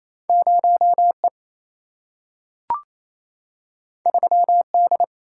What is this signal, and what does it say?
Morse code, heard as a keyed tone, about 28 wpm, 700 Hz. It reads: 0E 3D
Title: Single-Letter Single-Number 28wpm